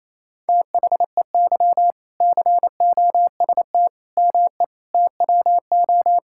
Morse code, heard as a keyed tone, about 28 wpm, 700 Hz. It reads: THEY COST ME TWO
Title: Sentences from Top 1000 Words - Review Part 5 - 28wpm